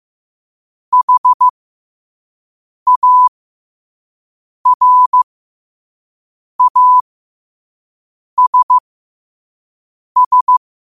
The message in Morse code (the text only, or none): HARASS